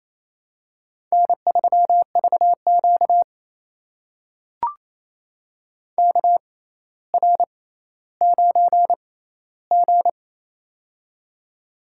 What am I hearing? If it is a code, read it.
Morse code, heard as a keyed tone, about 28 wpm, spaced as for 14 wpm, 700 Hz. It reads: N3VQ KR9G
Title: USA Call Signs - 4 Characters - 6x Character Spacing - 28wpm